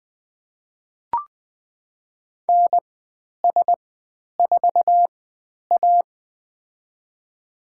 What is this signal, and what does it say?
Morse code, heard as a keyed tone, about 20 wpm, spaced as for 10 wpm, 700 Hz. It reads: NS4A
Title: 095 Lesson 16g - Letter c - Call Signs 20wpm